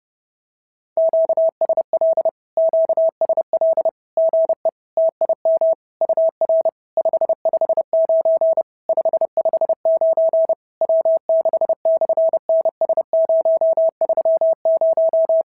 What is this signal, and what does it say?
Morse code, heard as a keyed tone, about 30 wpm, 650 Hz. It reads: QSL QSL GE TIM UR 559 559 W6/NS030
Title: SOTA QSO Practice 30wpm